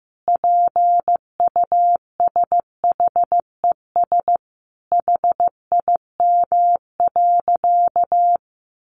Morse code, heard as a keyed tone, about 15 wpm, 700 Hz. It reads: PUSHES HIM.
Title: Additional Easy Phrases 15wpm